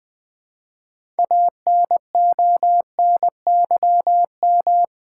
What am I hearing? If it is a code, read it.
Morse code, heard as a keyed tone, about 20 wpm, 700 Hz. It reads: ANONYM
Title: Build A Word - 3 Letters At A Time - 9-Letter Words - 20wpm